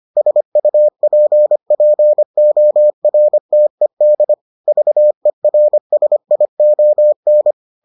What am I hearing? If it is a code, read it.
Morse code, heard as a keyed tone, about 25 wpm, 600 Hz. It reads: SUPPORTED VERSION